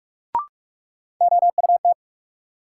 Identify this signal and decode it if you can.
Morse code, heard as a keyed tone, about 43 wpm, 700 Hz. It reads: OUT
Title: Sentences from Top 100 Words - New Word Drill - 45wpm